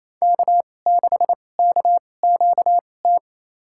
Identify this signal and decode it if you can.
Morse code, heard as a keyed tone, about 28 wpm, spaced as for 20 wpm, 700 Hz. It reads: K6KQT
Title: USA Call Signs - 5 Characters - 2x Character Spacing - 28wpm